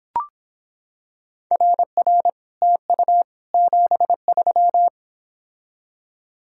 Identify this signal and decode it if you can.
Morse code, heard as a keyed tone, about 26 wpm, 700 Hz. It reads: RR TU 73